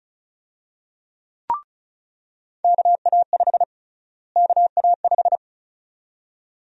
Morse code, heard as a keyed tone, about 35 wpm, 700 Hz. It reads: KA5 KA5
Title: USA Call Sign Prefixes - Repeated 2x - 35wpm